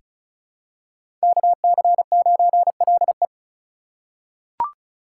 Morse code, heard as a keyed tone, about 36 wpm, 700 Hz. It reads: KC9LE
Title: USA Call Signs - 5 Characters - 3x Character Spacing - 35wpm